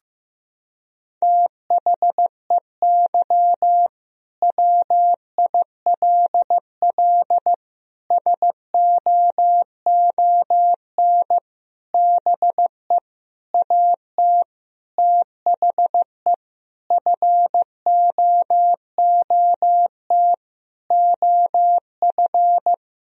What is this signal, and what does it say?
Morse code, heard as a keyed tone, about 15 wpm, 700 Hz. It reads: THEY WILL SOON BE AT THE FOOT OF
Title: Sentences from Top 1000 Words - Review Part 4 - 15wpm